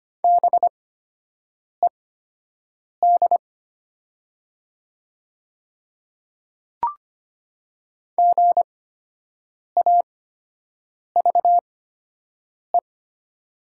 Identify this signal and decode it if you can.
Morse code, heard as a keyed tone, about 25 wpm, spaced as for 7 wpm, 700 Hz. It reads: BED GAVE
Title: 8x Character Spacing - Top 500 Words - 25wpm